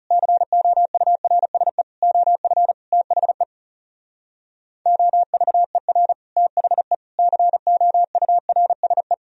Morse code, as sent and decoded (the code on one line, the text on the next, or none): -.-. --- ..- .-. ... / --- ..-. / - .... / --- ...- . .-. / - .... / -.-. --- ..- .-. ... .
COURSE OF THE OVER THE COURSE